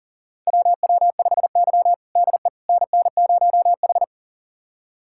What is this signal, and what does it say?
Morse code, heard as a keyed tone, about 40 wpm, 700 Hz. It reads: WW5Y DE NN0H
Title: USA Callsign DE Callsign - 4 Characters - 40wpm